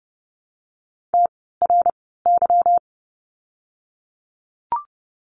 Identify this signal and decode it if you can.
Morse code, heard as a keyed tone, about 30 wpm, spaced as for 17 wpm, 700 Hz. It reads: TRY
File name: course / 30:10wpm / 230 Lesson 37f - Letter x - Words 30wpm